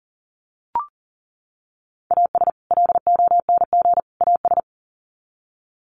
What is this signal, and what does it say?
Morse code, heard as a keyed tone, about 40 wpm, 700 Hz. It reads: AS LONG AS
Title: Sets of 3 Words 40wpm